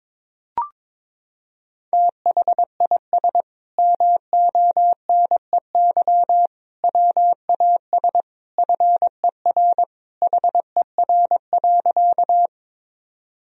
Morse code, heard as a keyed tone, about 22 wpm, 700 Hz. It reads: THIS MONEY WAS FER HER.
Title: Sentences from Top 400 Words and Abbreviations 22wpm